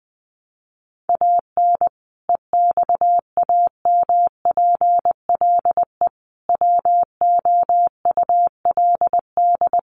Morse code, heard as a keyed tone, about 20 wpm, 700 Hz. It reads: AN EXAMPLE WOULD
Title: Sentences from Top 400 Words 20wpm